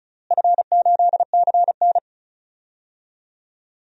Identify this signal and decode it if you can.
Morse code, heard as a keyed tone, about 34 wpm, 700 Hz. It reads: F8CN
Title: DXCC-100 Call Signs - 4 Characters - Repeated 3x - 35wpm